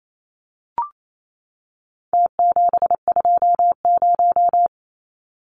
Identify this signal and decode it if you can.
Morse code, heard as a keyed tone, about 28 wpm, 700 Hz. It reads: T720
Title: Numbers - Contest Style - 4-Digit 28wpm